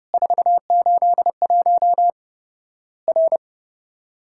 Morse code, heard as a keyed tone, about 30 wpm, 700 Hz. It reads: 481 R
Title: Worked All Europe DX Contest - QTC Exchange - 30wpm